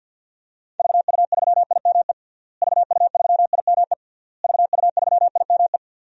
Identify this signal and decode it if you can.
Morse code, heard as a keyed tone, about 52 wpm, 700 Hz. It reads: VU3IME VU3IME VU3IME
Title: DXCC-100 Call Signs - 6 Characters - Repeated 3x - 50wpm